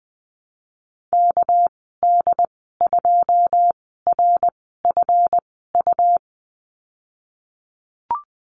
Morse code, heard as a keyed tone, about 20 wpm, spaced as for 15 wpm, 700 Hz. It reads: KD2RFU
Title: 231 Lesson 37g - Letter x - Call Signs 20wpm